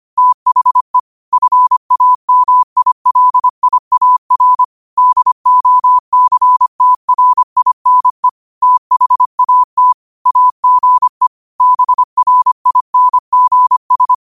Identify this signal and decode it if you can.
Morse code, heard as a keyed tone, about 25 wpm, 1000 Hz. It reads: THE FAMILIAR DOCTRINE THAT AGE BRINGS
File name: qotdfiles / 25wpm